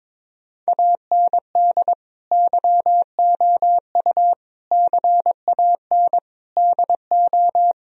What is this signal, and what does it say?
Morse code, heard as a keyed tone, about 22 wpm, 700 Hz. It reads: AND YOU CAN DO
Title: Sentences from Top 1000 Words - Review Part 1 - 22wpm